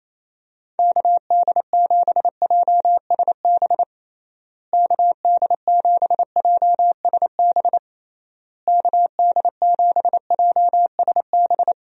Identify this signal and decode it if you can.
Morse code, heard as a keyed tone, about 28 wpm, 700 Hz. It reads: KD7JSB KD7JSB KD7JSB
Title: USA Call Signs - 6 Characters - Repeated 3x - 28wpm